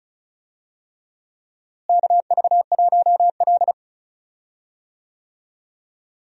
Morse code, heard as a keyed tone, about 34 wpm, 700 Hz. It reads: KV1L
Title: USA Call Signs - 4 Characters - Repeated 3x - 35wpm